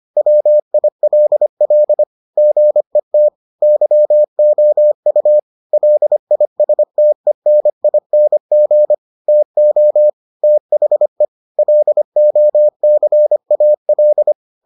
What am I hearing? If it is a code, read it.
Morse code, heard as a keyed tone, about 25 wpm, 600 Hz. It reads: WILL GET YOU LISTENING TO THE LOCAL